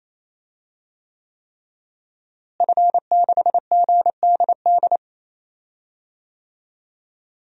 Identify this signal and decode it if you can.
Morse code, heard as a keyed tone, about 28 wpm, 700 Hz. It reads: F6GDD
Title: DXCC-100 Call Signs - 5 Characters - Repeated 3x - 28wpm